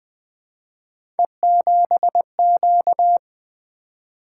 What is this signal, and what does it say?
Morse code, heard as a keyed tone, about 20 wpm, 700 Hz. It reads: E7Q
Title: Three Letter-Number Combinations - 20wpm